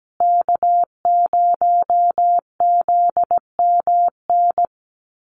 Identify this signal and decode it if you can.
Morse code, heard as a keyed tone, about 17 wpm, 700 Hz. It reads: K0ZMN
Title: USA Call Signs Repeated 2x 17wpm